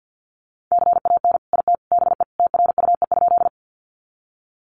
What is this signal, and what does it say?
Morse code, heard as a keyed tone, about 50 wpm, 700 Hz. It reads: CAN IT BE TRUE?
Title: Sentences from Top 400 Words 50wpm